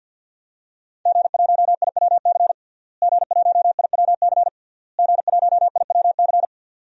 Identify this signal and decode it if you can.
Morse code, heard as a keyed tone, about 52 wpm, 700 Hz. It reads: G1IWC G1IWC G1IWC
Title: DXCC-100 Call Signs - 5 Characters - Repeated 3x - 50wpm